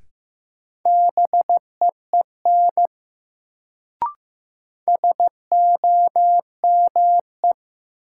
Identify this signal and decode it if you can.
Morse code, heard as a keyed tone, about 15 wpm, 700 Hz. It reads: BEEN SOME
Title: Sets of 2 Words - New Word Drill - 15wpm